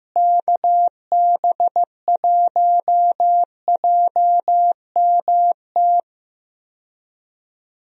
Morse code, heard as a keyed tone, about 15 wpm, 700 Hz. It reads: KB1JMT
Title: USA Call Signs - 6 Characters - Repeated 3x - 15wpm